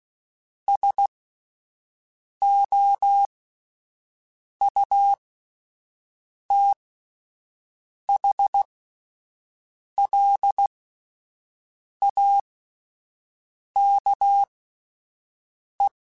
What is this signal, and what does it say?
Morse code, heard as a keyed tone, about 16 wpm, spaced as for 5 wpm, 800 Hz. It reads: SOUTHLAKE